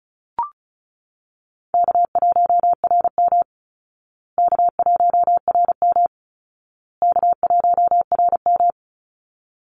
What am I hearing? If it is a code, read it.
Morse code, heard as a keyed tone, about 34 wpm, 700 Hz. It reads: K1RM K1RM K1RM
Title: Call Signs Repeated 2x - 35wpm